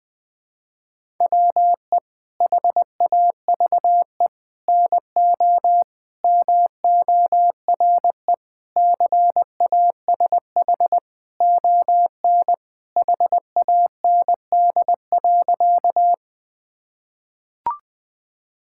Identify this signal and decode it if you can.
Morse code, heard as a keyed tone, about 20 wpm, 700 Hz. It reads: WE HAVE NO MORE CASH ON HAND.
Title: Sentences from Top 1000 Words - Review Part 5 - 20wpm